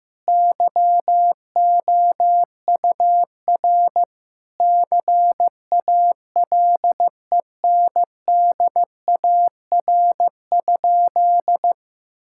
Morse code, heard as a keyed tone, about 15 wpm, 700 Hz. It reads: YOUR CALENDAR?
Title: Sentences from Top 1600 Words 15wpm